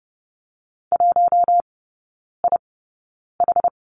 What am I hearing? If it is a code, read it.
Morse code, heard as a keyed tone, about 30 wpm, spaced as for 9 wpm, 700 Hz. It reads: 1IH
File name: 081 Lesson 14e - Number 2 - Three Characters 30wpm